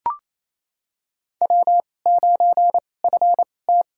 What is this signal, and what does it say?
Morse code, heard as a keyed tone, about 28 wpm, spaced as for 20 wpm, 700 Hz. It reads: W9FT
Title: USA Call Signs - 2x Character Spacing - 28wpm